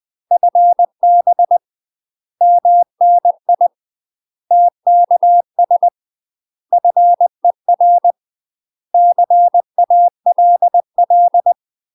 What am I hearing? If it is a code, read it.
Morse code, heard as a keyed tone, about 20 wpm, 700 Hz. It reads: FB MNI TKS FER CALL
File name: UK MASTER QSOs TRAINING / 20WPM / UK MASTER QSOs 0023